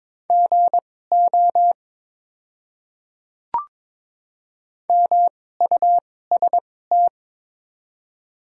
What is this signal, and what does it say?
Morse code, heard as a keyed tone, about 22 wpm, spaced as for 16 wpm, 700 Hz. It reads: GO MUST
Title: 2x Character Spacing - Top 300 Words - 22wpm